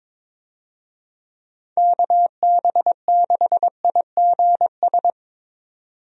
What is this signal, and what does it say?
Morse code, heard as a keyed tone, about 22 wpm, 700 Hz. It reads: KB6IGS
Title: USA Call Signs - 6 Characters - 22wpm